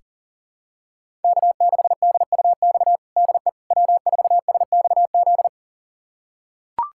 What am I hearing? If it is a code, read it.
Morse code, heard as a keyed tone, about 37 wpm, 700 Hz. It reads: K6DUX DE W4SXZ
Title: USA Callsign DE Callsign - 5 Characters - 40wpm